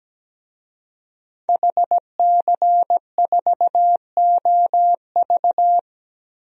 Morse code, heard as a keyed tone, about 17 wpm, 700 Hz. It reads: HC4OV
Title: DXCC-100 Call Signs - 5 Characters - 17wpm